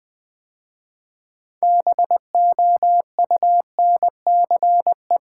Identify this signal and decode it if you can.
Morse code, heard as a keyed tone, about 20 wpm, 700 Hz. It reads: BOUNCE
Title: 6 Letter Words - First 3 Letters Given - 20wpm